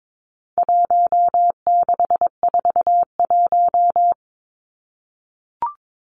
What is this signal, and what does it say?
Morse code, heard as a keyed tone, about 22 wpm, 700 Hz. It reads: 1641
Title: Numbers - Contest Style - 4-Digit 22wpm